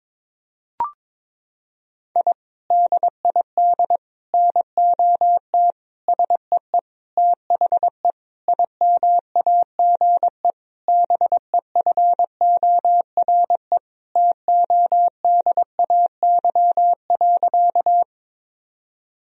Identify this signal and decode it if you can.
Morse code, heard as a keyed tone, about 22 wpm, 700 Hz. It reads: I DID NOT SEE THE IMAGE BEFORE TODAY.
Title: Sentences from Top 1000 Words - Review Part 4 - 22wpm